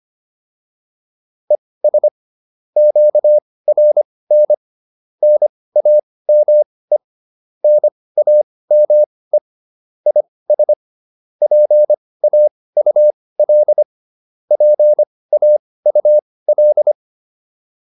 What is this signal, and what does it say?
Morse code, heard as a keyed tone, about 25 wpm, spaced as for 18 wpm, 600 Hz. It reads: ES QRN NAME NAME IS PAUL PAUL
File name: QSO408 18